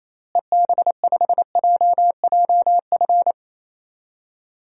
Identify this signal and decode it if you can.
Morse code, heard as a keyed tone, about 28 wpm, 700 Hz. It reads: EB5JJF
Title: DXCC-100 Call Signs - 6 Characters - Repeated 2x - 28wpm